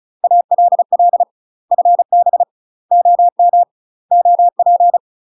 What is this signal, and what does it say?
Morse code, heard as a keyed tone, about 36 wpm, 700 Hz. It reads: ALL FB OM OP
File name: UK MASTER QSOs TRAINING / 35WPM / UK MASTER QSOs 0007